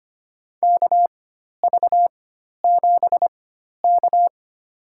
Morse code, heard as a keyed tone, about 25 wpm, spaced as for 12 wpm, 700 Hz. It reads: KV7K
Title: USA Call Signs - 4 Characters - 4x Character Spacing - 25wpm